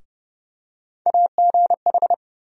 Morse code, heard as a keyed tone, about 29 wpm, 700 Hz. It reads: AGH
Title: USA Call Sign Postfixes - 3 Letters 30wpm